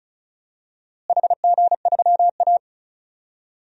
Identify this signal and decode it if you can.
Morse code, heard as a keyed tone, about 34 wpm, 700 Hz. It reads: HG3A
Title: DXCC-100 Call Signs - 4 Characters - Repeated 2x - 35wpm